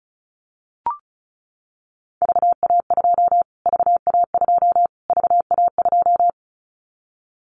Morse code, heard as a keyed tone, about 34 wpm, 700 Hz. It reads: VA2 VA2 VA2
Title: Callsign Prefixes for Top 25 DXCC - Repeated 3x - 35wpm